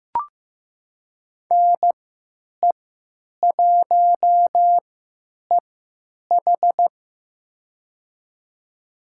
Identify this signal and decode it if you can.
Morse code, heard as a keyed tone, about 15 wpm, spaced as for 8 wpm, 700 Hz. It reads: NE1EH
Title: USA Call Signs - 3x Character Spacing - 15wpm